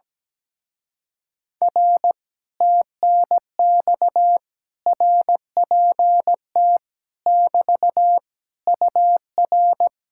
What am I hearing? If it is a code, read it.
Morse code, heard as a keyed tone, about 17 wpm, 700 Hz. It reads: R TNX RPT <BT> UR
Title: Speed-Racing Common QSO Phrases Encore - 17wpm